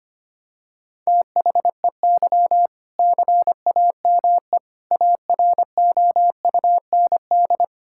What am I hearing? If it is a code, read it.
Morse code, heard as a keyed tone, about 25 wpm, 700 Hz. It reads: THEY CAME AROUND
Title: Sentences from Top 1600 Words 25wpm